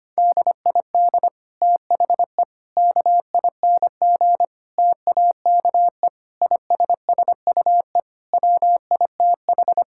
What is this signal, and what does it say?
Morse code, heard as a keyed tone, about 25 wpm, 700 Hz. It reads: DID THE KING TAKE ISSUE WITH
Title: Sentences from Top 1000 Words - Review Part 6 - 25wpm